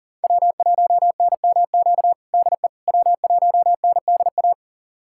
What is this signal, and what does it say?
Morse code, heard as a keyed tone, about 40 wpm, 700 Hz. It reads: W1NMQ DE W1NDA
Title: USA Callsign DE Callsign 40wpm